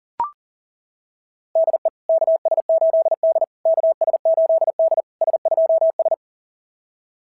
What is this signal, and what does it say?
Morse code, heard as a keyed tone, about 39 wpm, 690 Hz. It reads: DE KS8D KS8D S2S
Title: SOTA QSO Practice 40wpm